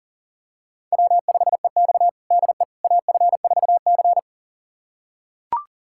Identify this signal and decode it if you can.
Morse code, heard as a keyed tone, about 40 wpm, 700 Hz. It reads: W5EX DE AF4C
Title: USA Callsign DE Callsign - 4 Characters - 40wpm